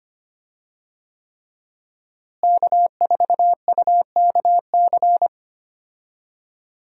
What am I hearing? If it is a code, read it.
Morse code, heard as a keyed tone, about 25 wpm, 700 Hz. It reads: K4UKC
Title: USA Call Signs - 5 Characters - Repeated 3x - 25wpm